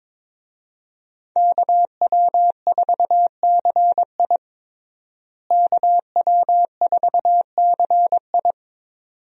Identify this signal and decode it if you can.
Morse code, heard as a keyed tone, about 22 wpm, 700 Hz. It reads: KW4CI KW4CI
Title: USA Call Signs - 5 Characters - Repeated 2x - 22wpm